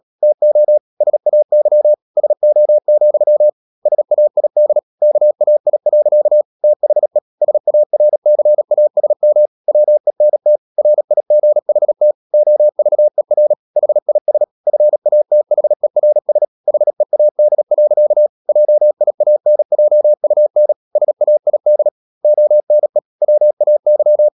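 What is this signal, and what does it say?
Morse code, heard as a keyed tone, about 37 wpm, 600 Hz. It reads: TO SAY SO, SAID KAI. THE SARCASM WENT RIGHT OVER HIS FATHERS HEAD. JIANJUN SAID ONE WAY
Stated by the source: Morse code practice at 35–39 wpm sending English text of Ken Follett's novel Never, from CW med Gnister.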